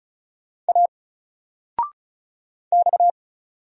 Morse code, heard as a keyed tone, about 34 wpm, 700 Hz. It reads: A X
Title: Single Letter-Number 35wpm